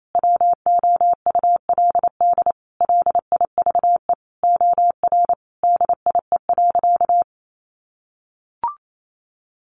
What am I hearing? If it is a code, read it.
Morse code, heard as a keyed tone, about 28 wpm, 700 Hz. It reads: WOULD LIVE OR DIE.
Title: Sentences from Top 1300 Words 28wpm